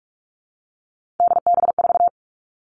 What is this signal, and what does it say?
Morse code, heard as a keyed tone, about 44 wpm, 700 Hz. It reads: DB4
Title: All Cut Numbers - 3-Digit 45wpm